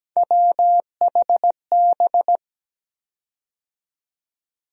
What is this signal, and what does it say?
Morse code, heard as a keyed tone, about 17 wpm, 700 Hz. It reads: WHB
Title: USA Call Sign Postfixes - 3 Letters 17wpm